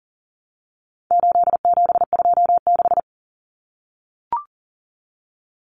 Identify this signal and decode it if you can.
Morse code, heard as a keyed tone, about 40 wpm, 700 Hz. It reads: 8726
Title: Common Cut Numbers - 4-Digit 40wpm